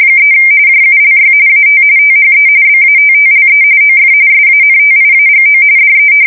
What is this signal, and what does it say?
RTTY, about 45.45 baud. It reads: 3 JULY AT THE BATTLE OF SANTIAGO DE
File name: fsk170 / battleships